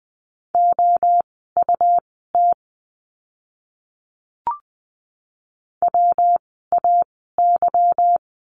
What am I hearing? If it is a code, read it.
Morse code, heard as a keyed tone, about 20 wpm, spaced as for 14 wpm, 700 Hz. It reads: OUT WAY
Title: 230 Lesson 37f - Letter x - Words 20wpm